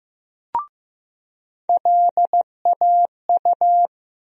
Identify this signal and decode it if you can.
Morse code, heard as a keyed tone, about 15 wpm, 700 Hz. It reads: LAU
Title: USA Call Sign Postfixes - 3 Letters 15wpm